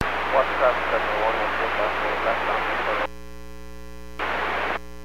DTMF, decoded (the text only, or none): none